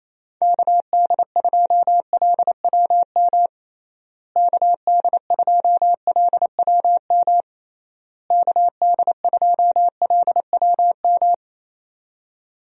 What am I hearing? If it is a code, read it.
Morse code, heard as a keyed tone, about 28 wpm, 700 Hz. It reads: KD2LWM KD2LWM KD2LWM
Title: USA Call Signs - 6 Characters - Repeated 3x - 28wpm